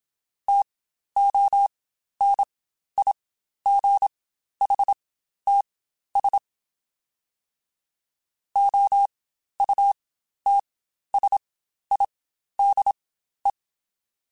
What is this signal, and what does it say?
Morse code, heard as a keyed tone, about 27 wpm, spaced as for 12 wpm, 800 Hz. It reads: TONIGHTS OUTSIDE
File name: Spaced Characters / Spaced Characters-mp3s / Weather 5 1 sp @25